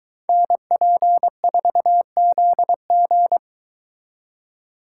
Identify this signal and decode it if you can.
Morse code, heard as a keyed tone, about 23 wpm, 700 Hz. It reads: NP4ZG